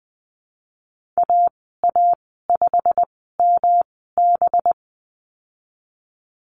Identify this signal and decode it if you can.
Morse code, heard as a keyed tone, about 20 wpm, spaced as for 14 wpm, 700 Hz. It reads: AA5MB
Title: 193 Lesson 31g - Letter v - Call Signs 20wpm